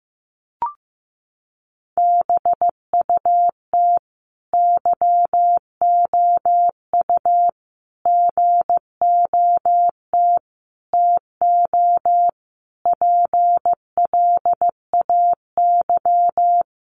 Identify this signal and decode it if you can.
Morse code, heard as a keyed tone, about 14 wpm, 700 Hz. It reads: BUT YOU GOT TO PLAY
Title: Sentences from Top 300 Words 15wpm